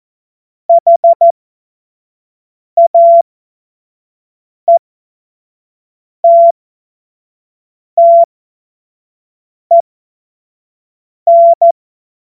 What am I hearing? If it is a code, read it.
Morse code, heard as a keyed tone, about 14 wpm, spaced as for 5 wpm, 680 Hz. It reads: HAETTEN
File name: Morsekurs / Witze / Witz 11. Mai 2021, 25 F70